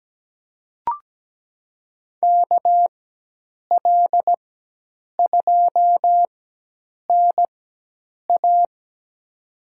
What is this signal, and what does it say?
Morse code, heard as a keyed tone, about 17 wpm, spaced as for 8 wpm, 700 Hz. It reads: KL2NA
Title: USA Call Signs - 5 Characters - 4x Character Spacing - 17wpm